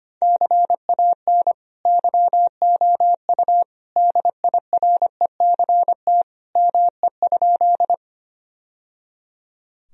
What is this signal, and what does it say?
Morse code, heard as a keyed tone, about 25 wpm, 700 Hz. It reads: CAN YOU DIRECT ME?
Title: Sentences from Top 1000 Words - Review Part 2 - 25wpm